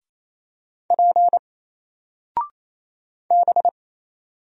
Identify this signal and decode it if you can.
Morse code, heard as a keyed tone, about 27 wpm, 700 Hz. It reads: P B
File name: Single-Letter-Number-28wpm